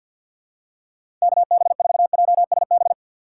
Morse code, heard as a keyed tone, about 52 wpm, 700 Hz. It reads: KB4JSB
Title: USA Call Signs - 6 Characters - 3x Character Spacing - 50wpm